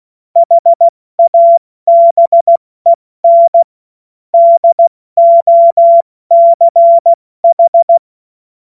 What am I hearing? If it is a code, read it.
Morse code, heard as a keyed tone, about 16 wpm, 680 Hz. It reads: HABEN DOCH
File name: Morsekurs / Witze / Witz 29. November 2022 70 F80